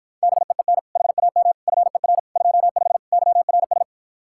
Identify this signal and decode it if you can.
Morse code, heard as a keyed tone, about 52 wpm, 700 Hz. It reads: BEEN HAM FER 25 YRS